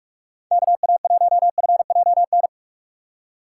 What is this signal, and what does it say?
Morse code, heard as a keyed tone, about 43 wpm, 700 Hz. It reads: KA1FJN
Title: USA Call Signs - 6 Characters - 6x Character Spacing - 45wpm